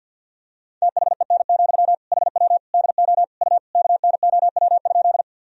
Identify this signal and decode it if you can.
Morse code, heard as a keyed tone, about 52 wpm, 700 Hz. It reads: THEN, HW DO U KNOW?
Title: Sentences from Top 100 Words and Abbreviations 50wpm